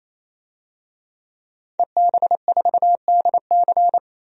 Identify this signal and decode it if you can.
Morse code, heard as a keyed tone, about 28 wpm, 700 Hz. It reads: EB4DC
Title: DXCC-100 Call Signs - 5 Characters - 28wpm